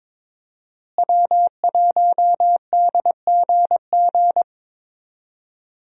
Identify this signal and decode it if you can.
Morse code, heard as a keyed tone, about 22 wpm, 700 Hz. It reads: W1DGG
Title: USA Call Signs - 5 Characters - Repeated 3x - 22wpm